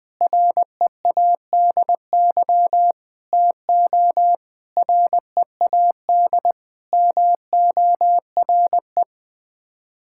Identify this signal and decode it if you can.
Morse code, heard as a keyed tone, about 20 wpm, 700 Hz. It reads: READY TO READ MORE
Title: Sets of 6 Words - 20wpm